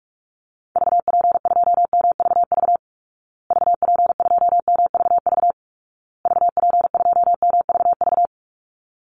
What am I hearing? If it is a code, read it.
Morse code, heard as a keyed tone, about 44 wpm, 700 Hz. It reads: VP2MVV VP2MVV VP2MVV
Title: Call Signs Repeated 2x - 45wpm